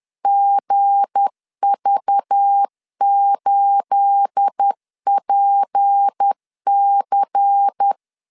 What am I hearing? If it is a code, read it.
Morse code, heard as a keyed tone, about 10 wpm, 800 Hz. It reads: GV8PC
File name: LEZIONI / 9 2867 10wpm 2733